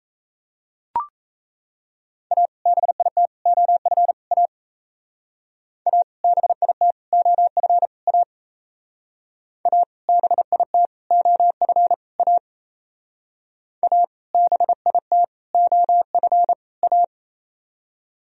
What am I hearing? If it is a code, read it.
Morse code, heard as a keyed tone, about 34 wpm, 700 Hz. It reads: A BIT OF A A BIT OF A A BIT OF A A BIT OF A
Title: Speed-Racing - Sets of 4 Words 28wpm